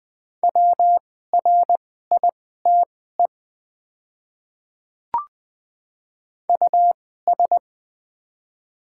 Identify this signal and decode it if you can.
Morse code, heard as a keyed tone, about 20 wpm, spaced as for 15 wpm, 700 Hz. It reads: WRITE US
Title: 2x Character Spacing - Top 200 Words - 20wpm